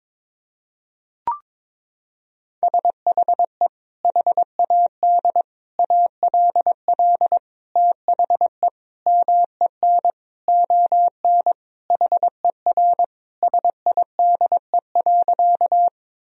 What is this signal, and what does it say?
Morse code, heard as a keyed tone, about 21 wpm, 700 Hz. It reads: SHE HAD ALL THE MEN ON HER SIDE.
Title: Sentences from Top 1000 Words - Review Part 1 - 22wpm